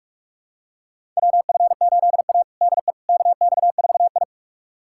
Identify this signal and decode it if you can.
Morse code, heard as a keyed tone, about 47 wpm, 700 Hz. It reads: WF8A DE KX4I
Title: USA Callsign DE Callsign - 4 Characters - 45wpm